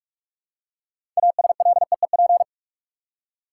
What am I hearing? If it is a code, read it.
Morse code, heard as a keyed tone, about 46 wpm, 700 Hz. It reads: ASLEEP
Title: Six Letter Words - 45wpm